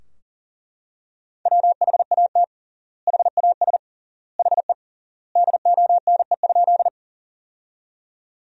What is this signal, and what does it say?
Morse code, heard as a keyed tone, about 40 wpm, 700 Hz. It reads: WHAT HAS HE DONE?
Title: 3x Word-Spacing - Sentences from Top 500 Words - 40wpm